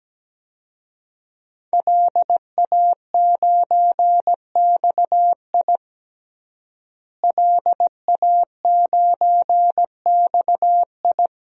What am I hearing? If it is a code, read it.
Morse code, heard as a keyed tone, about 17 wpm, 700 Hz. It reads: LA9XI LA9XI
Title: DXCC-100 Call Signs - 5 Characters - Repeated 3x - 17wpm